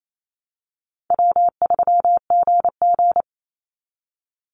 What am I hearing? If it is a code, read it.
Morse code, heard as a keyed tone, about 28 wpm, 700 Hz. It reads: W3GG
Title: Easy Call Signs Repeated 2x - 28wpm